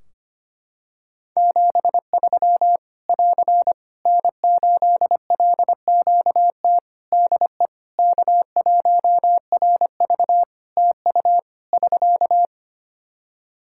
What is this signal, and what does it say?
Morse code, heard as a keyed tone, about 25 wpm, 700 Hz. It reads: 73 <AR> N8LQT DE K1RV TU <SK>
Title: K1USN SST QSO Practice 25wpm